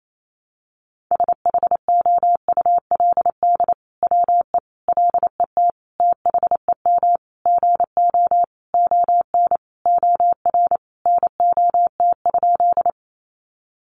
Morse code, heard as a keyed tone, about 28 wpm, 700 Hz. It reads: SHOULD WE LET THEM GO ON OR NOT?
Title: Sentences from Top 600 Words 28wpm